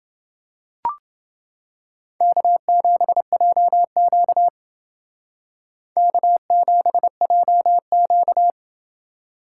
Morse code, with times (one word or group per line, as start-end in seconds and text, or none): 2.2-4.5 K7JQ
6.0-8.5 K7JQ